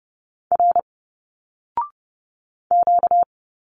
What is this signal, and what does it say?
Morse code, heard as a keyed tone, about 30 wpm, 700 Hz. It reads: R Q
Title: Single Letters 30wpm